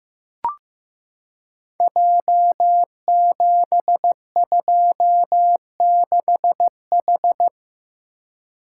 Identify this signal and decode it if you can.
Morse code, heard as a keyed tone, about 15 wpm, 700 Hz. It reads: J726H
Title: Code Groups - 5 Letter:Numbers - 15wpm